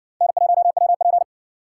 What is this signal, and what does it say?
Morse code, heard as a keyed tone, about 60 wpm, 700 Hz. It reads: N1WP